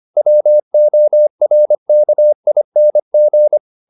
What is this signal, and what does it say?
Morse code, heard as a keyed tone, about 25 wpm, 600 Hz. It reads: WORKING
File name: the old man and the sea part 1 25wpm